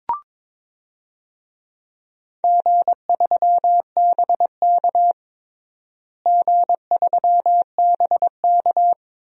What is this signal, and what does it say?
Morse code, heard as a keyed tone, about 22 wpm, 700 Hz. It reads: G3BK G3BK
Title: DXCC-100 Call Signs - 4 Characters - Repeated 3x - 22wpm